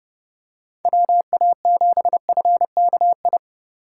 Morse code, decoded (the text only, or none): WA7FKI